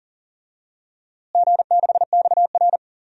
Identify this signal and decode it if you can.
Morse code, heard as a keyed tone, about 40 wpm, 700 Hz. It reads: G6XR
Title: DXCC-100 Call Signs - 4 Characters - Repeated 2x - 40wpm